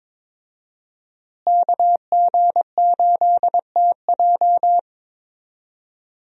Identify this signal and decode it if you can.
Morse code, heard as a keyed tone, about 22 wpm, 700 Hz. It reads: KG8TJ
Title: USA Call Signs - 5 Characters - Repeated 2x - 22wpm